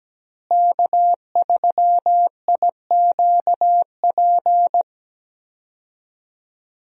Morse code, heard as a keyed tone, about 17 wpm, 700 Hz. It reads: K3IQP
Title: USA Call Signs - 5 Characters - Repeated 3x - 17wpm